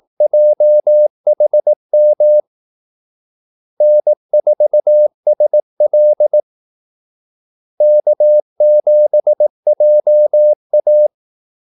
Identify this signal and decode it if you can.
Morse code, heard as a keyed tone, about 18 wpm, 600 Hz. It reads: JHM N4SL K7JA
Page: Kaldesignaler 18 wpm